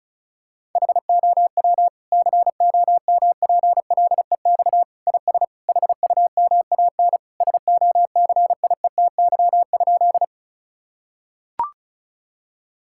Morse code, heard as a keyed tone, about 34 wpm, 700 Hz. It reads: HOW COMPLEX IS HUMAN SOCIETY?
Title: Sentences from Top 1000 Words - Review Part 6 - 35wpm